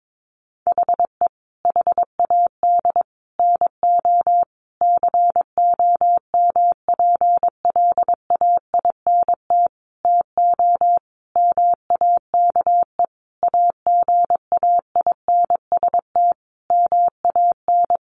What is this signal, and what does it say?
Morse code, heard as a keyed tone, about 22 wpm, 700 Hz. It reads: HE HAD NO COMPLAINT TO MAKE AGAINST MAN
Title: Sentences from Top 1400 Words 22wpm